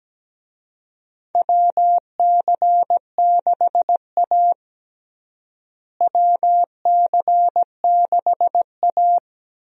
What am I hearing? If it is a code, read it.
Morse code, heard as a keyed tone, about 17 wpm, 700 Hz. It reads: WC6A WC6A
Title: USA Call Signs - 4 Characters - Repeated 3x - 17wpm